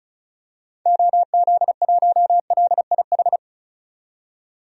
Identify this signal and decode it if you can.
Morse code, heard as a keyed tone, about 36 wpm, 700 Hz. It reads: OZ1LIH
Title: DXCC-100 Call Signs - 6 Characters - 35wpm